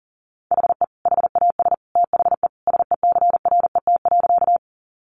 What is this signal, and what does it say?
Morse code, heard as a keyed tone, about 38 wpm, 700 Hz. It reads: HE HAS THE SECRET.
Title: Sentences from Top 1200 Words 40wpm